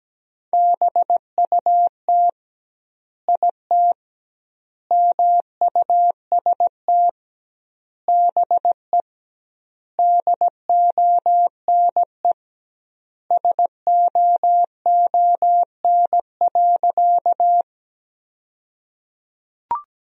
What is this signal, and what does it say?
Morse code, heard as a keyed tone, about 17 wpm, 700 Hz. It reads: BUT IT MUST BE DONE SOON.
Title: 2x Word-Spacing - Sentences from Top 500 Words - 17wpm